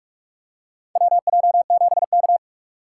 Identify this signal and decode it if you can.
Morse code, heard as a keyed tone, about 46 wpm, 700 Hz. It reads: WJ7K
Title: USA Call Signs - 4 Characters - 2x Character Spacing - 45wpm